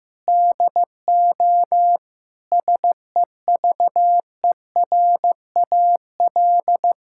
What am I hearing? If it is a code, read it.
Morse code, heard as a keyed tone, about 15 wpm, 700 Hz. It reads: DO SEVERAL
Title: Sentences from Top 1000 Words - Review Part 6 - 15wpm